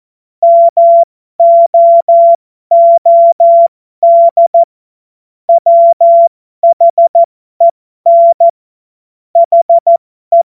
Morse code, heard as a keyed tone, about 14 wpm, spaced as for 12 wpm, 680 Hz. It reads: MOOD WHEN HE